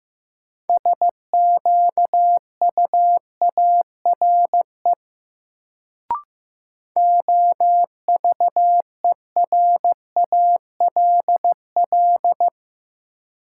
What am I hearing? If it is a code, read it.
Morse code, heard as a keyed tone, about 15 wpm, 700 Hz. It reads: SQUARE OVERALL
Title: Sentences from Top 1000 Words - Review Part 7 - New Word Drill - 15wpm